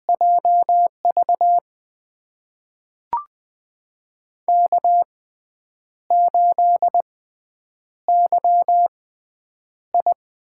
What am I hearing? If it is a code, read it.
Morse code, heard as a keyed tone, about 20 wpm, spaced as for 8 wpm, 700 Hz. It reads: JV K8YI
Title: USA Call Signs - 4 Characters - 6x Character Spacing - 20wpm